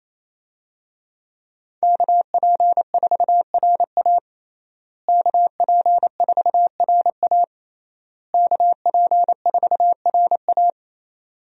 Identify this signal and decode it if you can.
Morse code, heard as a keyed tone, about 28 wpm, 700 Hz. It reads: KP4RA KP4RA KP4RA
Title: DXCC-100 Call Signs - 5 Characters - Repeated 3x - 28wpm